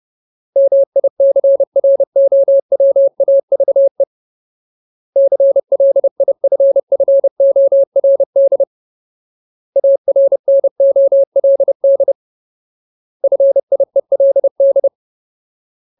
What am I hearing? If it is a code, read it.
Morse code, heard as a keyed tone, about 30 wpm, 550 Hz. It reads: MICROWAVE CLIFFORD ARNOLD FIELD
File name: Ultimate Word List / 30WPM Ultimate